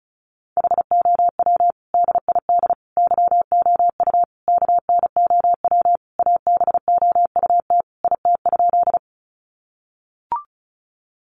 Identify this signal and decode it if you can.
Morse code, heard as a keyed tone, about 34 wpm, 700 Hz. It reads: HOW DID YOU KNOW ABOUT IT?
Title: Sentences from Top 100 Words 35wpm